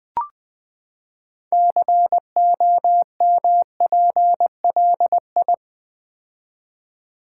Morse code, heard as a keyed tone, about 20 wpm, 700 Hz. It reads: COMPLI
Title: Build A Word - 6 Letters At A Time - 12-Letter Words - 20wpm